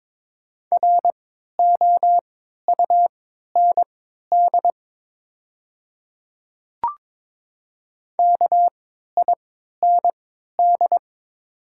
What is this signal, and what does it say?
Morse code, heard as a keyed tone, about 22 wpm, spaced as for 12 wpm, 690 Hz. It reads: ROUND KIND
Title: 3x Character Spacing - Top 200 Words - 22wpm